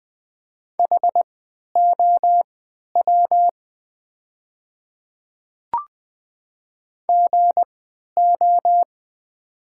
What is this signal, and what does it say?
Morse code, heard as a keyed tone, about 20 wpm, spaced as for 11 wpm, 700 Hz. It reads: HOW GO
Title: 3x Character Spacing - Top 100 Words - 20wpm